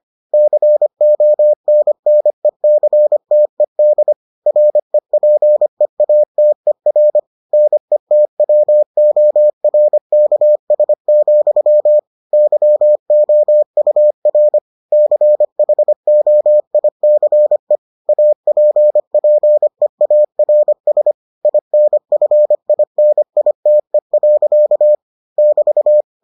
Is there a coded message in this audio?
Yes: Morse code from 0.3 to 26.0 s, heard as a keyed tone, about 25 wpm, 600 Hz. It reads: CONNECTED REPEATER NETWORKS, YOUR CHOICE APPEARS INFINITE. <BT>